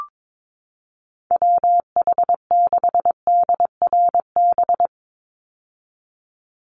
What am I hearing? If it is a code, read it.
Morse code, heard as a keyed tone, about 22 wpm, 700 Hz. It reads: WH6DRB
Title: USA Call Signs 22wpm